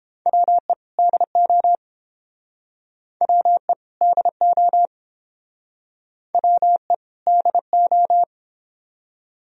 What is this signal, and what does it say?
Morse code, heard as a keyed tone, about 29 wpm, 700 Hz. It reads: WE DO WE DO WE DO